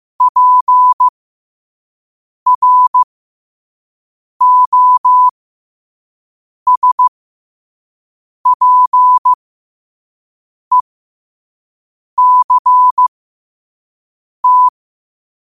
Morse code, heard as a keyed tone, about 15 wpm, spaced as for 5 wpm, 1000 Hz. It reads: PROSPECT
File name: qotdfiles / 5wpm